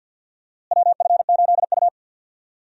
Morse code, heard as a keyed tone, about 49 wpm, 700 Hz. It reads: WF8U
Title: USA Call Signs - 4 Characters - Repeated 2x - 50wpm